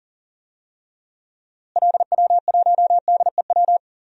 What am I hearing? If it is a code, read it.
Morse code, heard as a keyed tone, about 40 wpm, 700 Hz. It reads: LW1DEW
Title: DXCC-100 Call Signs - 6 Characters - Repeated 2x - 40wpm